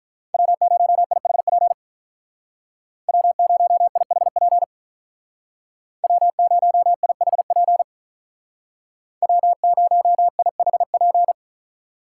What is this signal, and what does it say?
Morse code, heard as a keyed tone, about 46 wpm, 700 Hz. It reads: W0IHP W0IHP W0IHP W0IHP